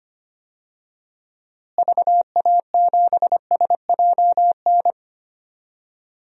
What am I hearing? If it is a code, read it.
Morse code, heard as a keyed tone, about 25 wpm, 700 Hz. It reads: VA7SJN
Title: DXCC-100 Call Signs - 6 Characters - Repeated 3x - 25wpm